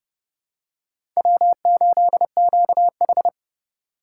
Morse code, heard as a keyed tone, about 30 wpm, 700 Hz. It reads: W8QH
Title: USA Call Signs - 4 Characters - 3x Character Spacing - 30wpm